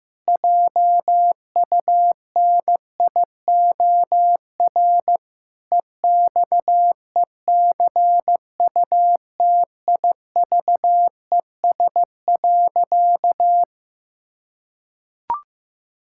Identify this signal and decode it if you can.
Morse code, heard as a keyed tone, about 15 wpm, 700 Hz. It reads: JUNIOR EXECUTIVES.
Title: Pangram Challenge v2 - 15wpm